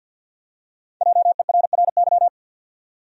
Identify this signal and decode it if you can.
Morse code, heard as a keyed tone, about 51 wpm, 700 Hz. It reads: JERRY
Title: USA Common Names - Max 5 Letters - 50wpm